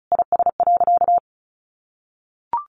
Morse code, heard as a keyed tone, about 34 wpm, 700 Hz. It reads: IS.
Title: Sentences from Top 1400 Words 35wpm